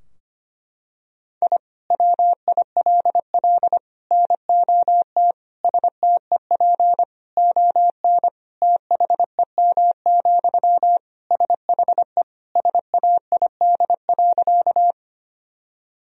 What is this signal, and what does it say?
Morse code, heard as a keyed tone, about 25 wpm, 700 Hz. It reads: I WILL NOT STEP ON THEM, SHE SAID.
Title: Sentences from Top 1000 Words - Review Part 2 - 25wpm